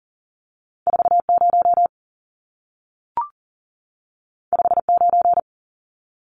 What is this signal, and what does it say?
Morse code, heard as a keyed tone, about 40 wpm, 700 Hz. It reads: 40 59
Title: Numbers - 2-Digit 40wpm